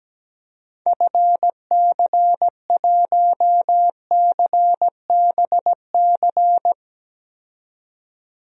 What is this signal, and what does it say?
Morse code, heard as a keyed tone, about 17 wpm, 700 Hz. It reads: FC1CBC
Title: DXCC-100 Call Signs - 6 Characters - 17wpm